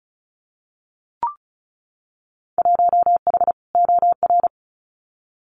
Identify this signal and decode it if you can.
Morse code, heard as a keyed tone, about 34 wpm, 700 Hz. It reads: 1H OR
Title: Winter Field Day Exchanges 35wpm